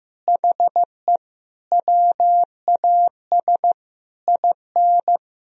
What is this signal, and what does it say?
Morse code, heard as a keyed tone, about 15 wpm, 700 Hz. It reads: HE WAS IN
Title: Sentences from Top 1000 Words - Review Part 4 - 15wpm